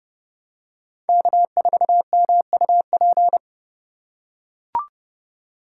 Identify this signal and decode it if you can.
Morse code, heard as a keyed tone, about 30 wpm, 700 Hz. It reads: K4MUP
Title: USA Call Signs - 5 Characters - 3x Character Spacing - 30wpm